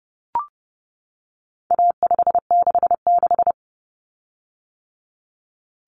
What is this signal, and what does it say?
Morse code, heard as a keyed tone, about 30 wpm, 700 Hz. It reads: A566
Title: Common Cut Numbers - 4-Digit 30wpm